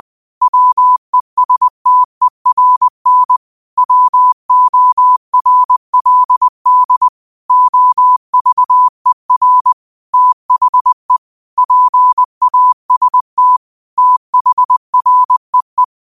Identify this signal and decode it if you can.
Morse code, heard as a keyed tone, about 20 wpm, 1000 Hz. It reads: WESTERN WORLD OVER THE PAST THREE